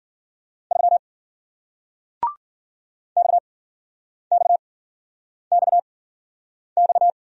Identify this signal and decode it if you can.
Morse code, heard as a keyed tone, about 49 wpm, spaced as for 9 wpm, 700 Hz. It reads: 3 XXXX
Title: Speed-Racing Single Letter-Number 40wpm